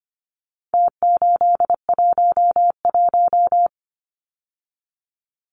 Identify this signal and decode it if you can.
Morse code, heard as a keyed tone, about 25 wpm, 700 Hz. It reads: T811
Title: Numbers - Contest Style - 4-Digit 25wpm